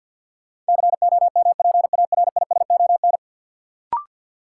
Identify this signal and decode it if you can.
Morse code, heard as a keyed tone, about 48 wpm, 700 Hz. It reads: COMPARISON
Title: Sentences from Top 1600 Words - New Word Drill - 50wpm